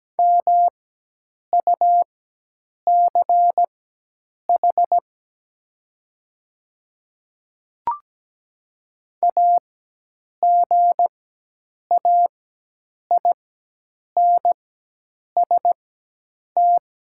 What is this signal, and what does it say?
Morse code, heard as a keyed tone, about 17 wpm, spaced as for 8 wpm, 700 Hz. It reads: MUCH AGAINST
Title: 4x Character Spacing - Top 500 Words - 17wpm